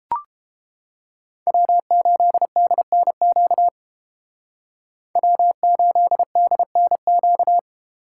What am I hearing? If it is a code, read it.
Morse code, heard as a keyed tone, about 29 wpm, 700 Hz. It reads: W8DNQ W8DNQ